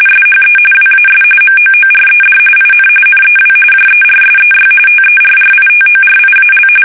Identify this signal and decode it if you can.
RTTY, about 45.45 baud. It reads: SHE WAS MOORED IN NEW JERSEY TERRITORIAL
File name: fsk850 / battleshipnj